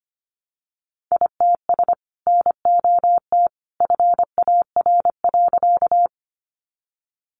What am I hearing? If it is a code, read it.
Morse code, heard as a keyed tone, about 25 wpm, 700 Hz. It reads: ITS NOT FAR.
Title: Sentences from Top 600 Words 25wpm